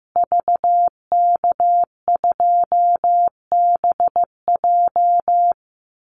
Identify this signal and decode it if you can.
Morse code, heard as a keyed tone, about 15 wpm, 700 Hz. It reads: VK2BJ
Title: Call Signs Repeated 2x - 15wpm